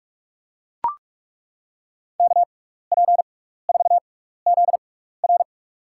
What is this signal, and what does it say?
Morse code, heard as a keyed tone, about 42 wpm, spaced as for 15 wpm, 700 Hz. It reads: KP4ZR
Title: USA Call Signs - 5 Characters - 6x Character Spacing - 45wpm